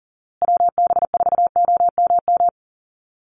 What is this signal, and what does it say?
Morse code, heard as a keyed tone, about 37 wpm, 700 Hz. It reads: WB4OMM
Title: Call Signs Repeated 2x - 40wpm